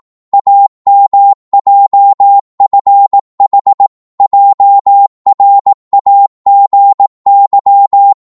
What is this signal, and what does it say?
Morse code, heard as a keyed tone, about 18 wpm, 800 Hz. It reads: AMJFH JRAGY
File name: Morse / ltr-18wpm